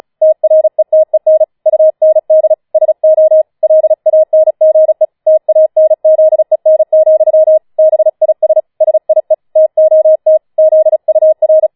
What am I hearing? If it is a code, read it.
Morse code, heard as a keyed tone, about 36 wpm, 630 Hz. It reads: TRETEN UND SO LANGE TANZEN, BIS SIE TOT ZUR
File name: Schneewittchen Teil6 36wpm